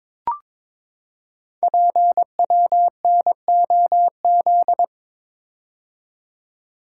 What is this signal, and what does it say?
Morse code, heard as a keyed tone, about 22 wpm, 700 Hz. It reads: PWNOZ
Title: Code Groups - 5 Letters - 22wpm